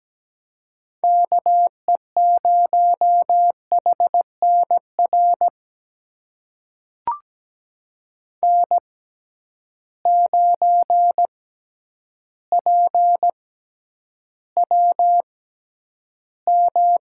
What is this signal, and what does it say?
Morse code, heard as a keyed tone, about 17 wpm, spaced as for 9 wpm, 700 Hz. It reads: KE0HNR N9PWM
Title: USA Call Signs - 6x Character Spacing - 17wpm